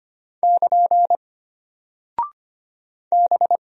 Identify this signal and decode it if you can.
Morse code, heard as a keyed tone, about 25 wpm, 700 Hz. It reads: <KN> B
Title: Single Character (Letter:Number:Punctuation:Prosign) Alt 25wpm